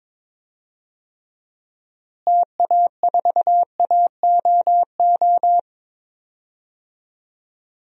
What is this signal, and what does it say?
Morse code, heard as a keyed tone, about 22 wpm, 700 Hz. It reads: TA4AOO
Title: DXCC-100 Call Signs - 6 Characters - Repeated 3x - 22wpm